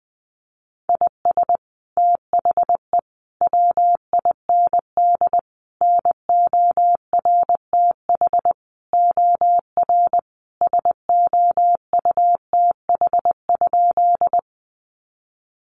Morse code, heard as a keyed tone, about 20 wpm, 700 Hz. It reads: IS THE WIND NORTH OR SOUTH?
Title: Sentences from Top 400 Words 20wpm